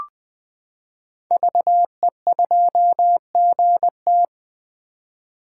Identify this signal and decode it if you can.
Morse code, heard as a keyed tone, about 20 wpm, 700 Hz. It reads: VE2GT
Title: 2023 SOTA Chaser Call Signs - 20wpm